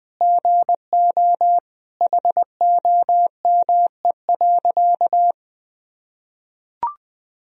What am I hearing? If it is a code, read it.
Morse code, heard as a keyed tone, about 20 wpm, 700 Hz. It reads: GO HOME.
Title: Sentences from Top 200 Words and Abbreviations 20wpm